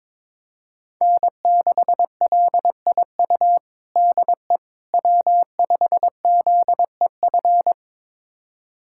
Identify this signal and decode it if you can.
Morse code, heard as a keyed tone, about 22 wpm, 700 Hz. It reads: N6LIU DE W5ZEF
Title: USA Callsign DE Callsign - 5 Characters - 22wpm